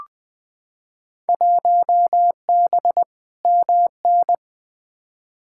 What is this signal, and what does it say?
Morse code, heard as a keyed tone, about 20 wpm, 700 Hz. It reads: 1B MN
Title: Field Day Exchanges Encore - 20wpm